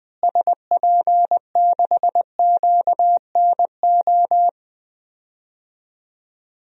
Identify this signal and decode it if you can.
Morse code, heard as a keyed tone, about 21 wpm, 700 Hz. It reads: SP6QNO
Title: DXCC-100 Call Signs - 6 Characters - Repeated 3x - 20wpm